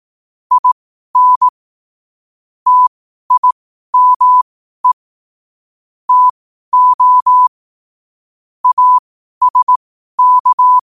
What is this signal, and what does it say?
Morse code, heard as a keyed tone, about 18 wpm, spaced as for 12 wpm, 1000 Hz. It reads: IN TIME TO ASK
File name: qotdfiles / 12wpm